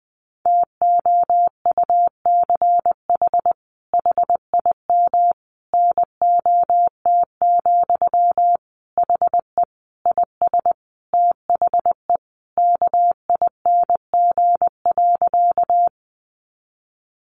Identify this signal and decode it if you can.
Morse code, heard as a keyed tone, about 19 wpm, 700 Hz. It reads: TOUCH HIM NOT, HE IS THE KING.
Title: Sentences from Top 1200 Words 20wpm